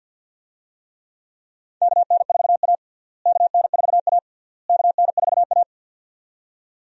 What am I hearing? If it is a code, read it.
Morse code, heard as a keyed tone, about 50 wpm, 700 Hz. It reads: KN4A KN4A KN4A
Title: USA Call Signs - 4 Characters - Repeated 3x - 50wpm